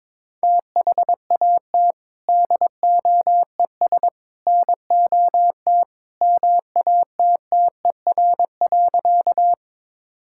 Morse code, heard as a keyed tone, about 21 wpm, 700 Hz. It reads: THAT DOES NOT MATTER.
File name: practice / 22wpm / Sentences with 3-Word Phrases - 22wpm